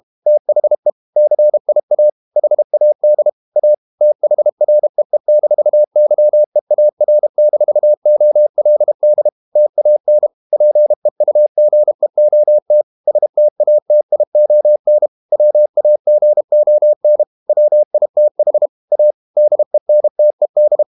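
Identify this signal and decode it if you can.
Morse code, heard as a keyed tone, about 32 wpm, 600 Hz. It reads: THE CIA HAD A THREE-YEAR-OLD TAN PEUGEOT STATION WAGON WITH A DENTED
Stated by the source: Morse code practice at 32 wpm from CW med Gnister, reading Ken Follett's novel Never in English.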